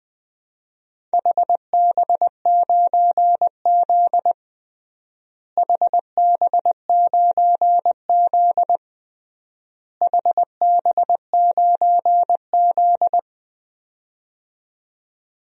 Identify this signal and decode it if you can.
Morse code, heard as a keyed tone, about 20 wpm, 700 Hz. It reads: HB9Z HB9Z HB9Z
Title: DXCC-100 Call Signs - 4 Characters - Repeated 3x - 20wpm